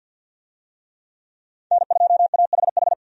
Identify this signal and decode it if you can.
Morse code, heard as a keyed tone, about 48 wpm, 700 Hz. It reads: N2AHH
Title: USA Call Signs - 5 Characters - 50wpm